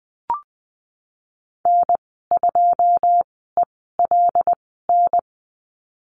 Morse code, heard as a keyed tone, about 20 wpm, spaced as for 15 wpm, 700 Hz. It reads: N2ELN
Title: 147 Lesson 24g - Letter y - Call Signs 20wpm